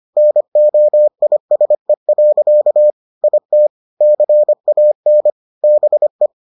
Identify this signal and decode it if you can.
Morse code, heard as a keyed tone, about 25 wpm, 600 Hz. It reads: NOISE. IT CAN BE